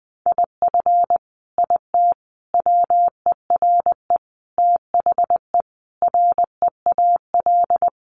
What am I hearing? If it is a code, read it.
Morse code, heard as a keyed tone, about 20 wpm, 700 Hz. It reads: IF IT WERE THE REAL
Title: Sentences from Top 1300 Words 20wpm